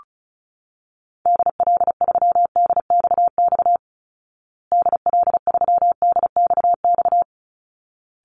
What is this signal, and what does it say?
Morse code, heard as a keyed tone, about 34 wpm, 700 Hz. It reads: DL3DXX DL3DXX
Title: Call Signs Repeated 3x - 35wpm